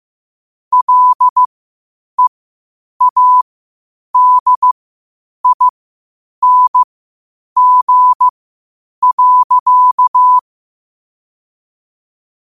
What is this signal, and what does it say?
Morse code, heard as a keyed tone, about 15 wpm, spaced as for 8 wpm, 1000 Hz. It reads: LEADING.
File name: qotdfiles / 8wpm